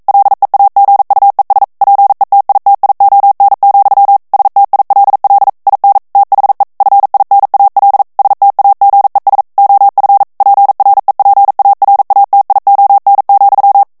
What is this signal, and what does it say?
Morse code, heard as a keyed tone, about 42 wpm, 800 Hz. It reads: LEAGUES PETITION, STILL IN THE FINAL STAGES OF PREPARATION,
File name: 40 mots minute